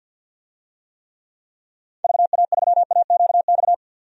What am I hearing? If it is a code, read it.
Morse code, heard as a keyed tone, about 45 wpm, 700 Hz. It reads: VA3AQX
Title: DXCC-100 Call Signs - 6 Characters - 50wpm